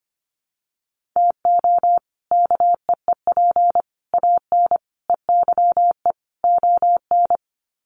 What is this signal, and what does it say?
Morse code, heard as a keyed tone, about 25 wpm, 700 Hz. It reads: TO KEEP AN EYE ON
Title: Sets of 5 Words Encore 25wpm